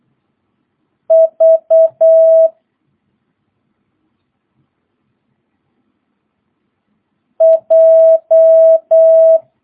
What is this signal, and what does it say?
Morse code, heard as a keyed tone, about 8 wpm, 660 Hz. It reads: V J
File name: jh8cbh / 2022cw